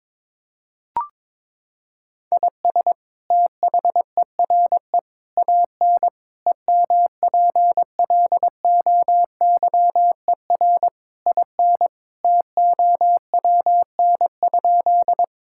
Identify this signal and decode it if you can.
Morse code, heard as a keyed tone, about 22 wpm, 700 Hz. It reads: IS THERE AN EMPLOYER IN TOWN?
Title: Sentences from Top 1600 Words 22wpm